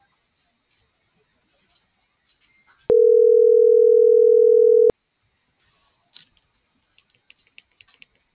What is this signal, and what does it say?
Ringback tone (North America), 440 + 480 Hz.